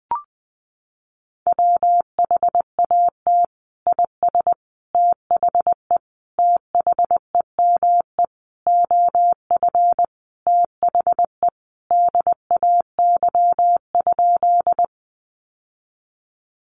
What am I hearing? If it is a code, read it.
Morse code, heard as a keyed tone, about 20 wpm, 700 Hz. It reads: WHAT IS THE THEME OF THE DAY?
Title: Sentences from Top 1100 Words 20wpm